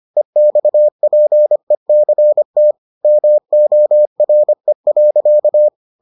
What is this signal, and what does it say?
Morse code, heard as a keyed tone, about 25 wpm, 600 Hz. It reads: EXPECT MORE.